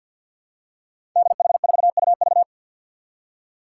Morse code, heard as a keyed tone, about 48 wpm, 700 Hz. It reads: DH4UV
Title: Five Challenging Letter-Number Combinations 50wpm